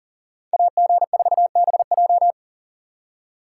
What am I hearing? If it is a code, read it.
Morse code, heard as a keyed tone, about 52 wpm, 700 Hz. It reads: AG4BJ